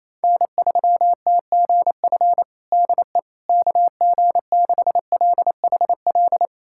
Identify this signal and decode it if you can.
Morse code, heard as a keyed tone, about 28 wpm, 700 Hz. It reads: N3TGF DE KG6LHL
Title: USA Callsign DE Callsign 28wpm